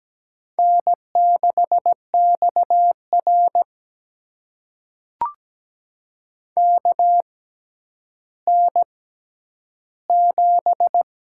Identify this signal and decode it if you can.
Morse code, heard as a keyed tone, about 17 wpm, spaced as for 10 wpm, 700 Hz. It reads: N6XR KN7
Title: USA Call Signs - 4 Characters - 6x Character Spacing - 17wpm